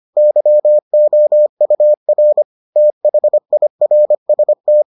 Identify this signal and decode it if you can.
Morse code, heard as a keyed tone, about 25 wpm, 600 Hz. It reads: YOUR THIRST